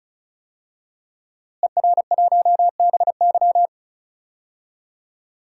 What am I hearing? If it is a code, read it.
Morse code, heard as a keyed tone, about 34 wpm, 700 Hz. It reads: ER1BY
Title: DXCC-100 Call Signs - 5 Characters - Repeated 2x - 35wpm